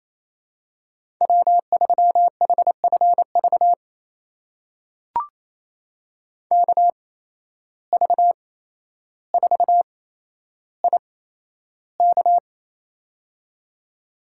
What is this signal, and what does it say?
Morse code, heard as a keyed tone, about 28 wpm, spaced as for 12 wpm, 700 Hz. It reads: W3HFV KV4IK
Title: USA Call Signs - 5 Characters - 8x Character Spacing - 28wpm